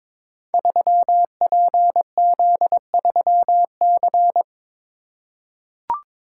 Morse code, heard as a keyed tone, about 22 wpm, 700 Hz. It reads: 3PZ3C
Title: Code Groups - 5 Letter:Numbers - 22wpm